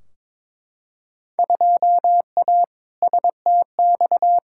Morse code, heard as a keyed tone, about 22 wpm, 700 Hz. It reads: 2A STX
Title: Field Day Exchanges - 22wpm